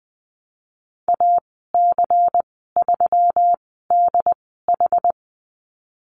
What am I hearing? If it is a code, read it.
Morse code, heard as a keyed tone, about 20 wpm, spaced as for 15 wpm, 700 Hz. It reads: AC3DH
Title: 128 Lesson 21g - Number 6 - Call Signs 20wpm